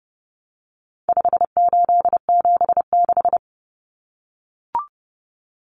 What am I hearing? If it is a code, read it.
Morse code, heard as a keyed tone, about 30 wpm, 700 Hz. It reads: 5876
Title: Numbers - 4-Digit 30wpm